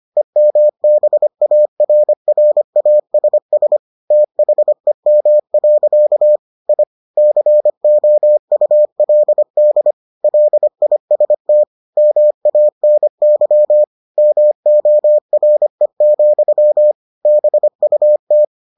Morse code, heard as a keyed tone, about 25 wpm, 600 Hz. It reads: EMBARRASS THEM. I COULD LIST MANY MORE, BUT